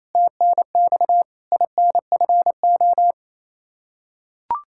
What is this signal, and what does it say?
Morse code, heard as a keyed tone, about 28 wpm, 700 Hz. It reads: TNX INFO
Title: Practice QSO - Exchange 3 - More Info - 28wpm